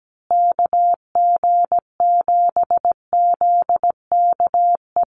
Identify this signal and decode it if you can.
Morse code, heard as a keyed tone, about 17 wpm, 700 Hz. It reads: KG7ZKE
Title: USA Call Signs Repeated 3x 17wpm